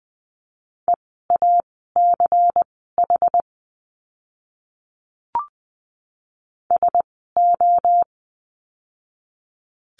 Morse code, heard as a keyed tone, about 20 wpm, spaced as for 15 wpm, 700 Hz. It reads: EACH SO
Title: 192 Lesson 31f - Letter v - Words 20wpm